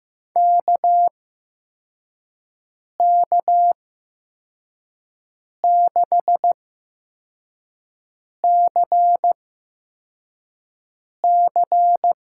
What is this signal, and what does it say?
Morse code, heard as a keyed tone, about 15 wpm, spaced as for 4 wpm, 700 Hz. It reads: KK6CC
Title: USA Call Signs - 6 Characters - 8x Character Spacing - 15wpm